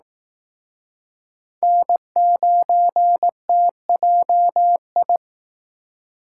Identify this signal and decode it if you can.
Morse code, heard as a keyed tone, about 19 wpm, 700 Hz. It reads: N9TJI